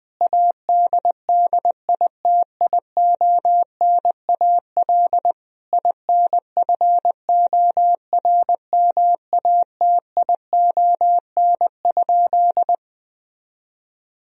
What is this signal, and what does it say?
Morse code, heard as a keyed tone, about 20 wpm, 700 Hz. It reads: ADDITIONAL INFORMATION?
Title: Sentences from Top 1000 Words - Review Part 4 - 20wpm